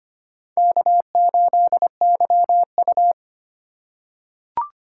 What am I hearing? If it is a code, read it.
Morse code, heard as a keyed tone, about 25 wpm, 700 Hz. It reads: K8YU
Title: USA Call Signs - 4 Characters - 8x Character Spacing - 25wpm